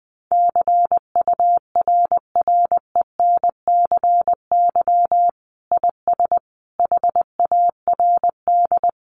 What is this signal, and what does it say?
Morse code, heard as a keyed tone, about 20 wpm, 700 Hz. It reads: CURRENCY IS HARD
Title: Sentences from Top 1500 Words 20wpm